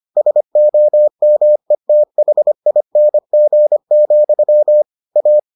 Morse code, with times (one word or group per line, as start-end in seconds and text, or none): 0.2-4.8 SOMETHING,
5.2-5.4 A